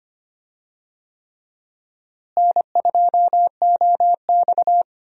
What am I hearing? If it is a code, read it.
Morse code, heard as a keyed tone, about 25 wpm, 700 Hz. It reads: N2OX
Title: USA Call Signs - 4 Characters - Repeated 3x - 25wpm